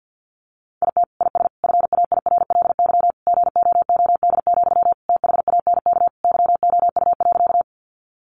Morse code, heard as a keyed tone, about 50 wpm, 700 Hz. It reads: IT IS FAIRLY GOOD, THANK YOU.
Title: Sentences from Top 1100 Words 50wpm